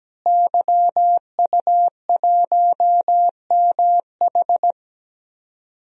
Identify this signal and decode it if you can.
Morse code, heard as a keyed tone, about 17 wpm, 700 Hz. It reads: YU1MH
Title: DXCC-100 Call Signs - 5 Characters - Repeated 3x - 17wpm